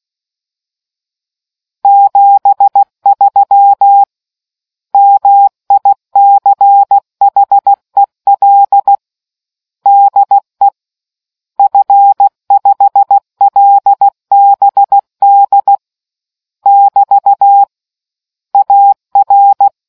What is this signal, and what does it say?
Morse code, heard as a keyed tone, about 16 wpm, 800 Hz. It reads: 73 MICHEL DE F5LBD <BT> AR